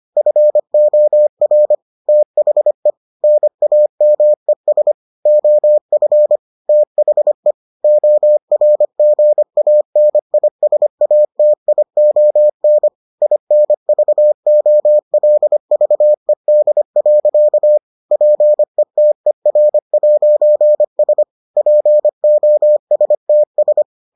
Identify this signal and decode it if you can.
Morse code, heard as a keyed tone, about 25 wpm, 600 Hz. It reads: FOR THE NAMES OF THE ORGANISATION INVOLVED. PETER'S POSTS